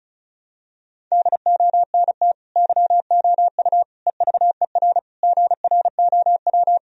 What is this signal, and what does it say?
Morse code, heard as a keyed tone, about 34 wpm, 700 Hz. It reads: DONT YOU EVER GROW
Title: Sentences from Top 1000 Words - Review Part 1 - 35wpm